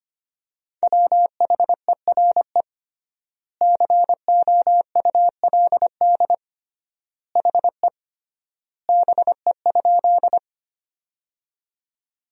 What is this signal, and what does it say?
Morse code, heard as a keyed tone, about 25 wpm, 700 Hz. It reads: WHERE COULD HE BE?
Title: 3x Word-Spacing - Sentences from Top 200 Words - 25wpm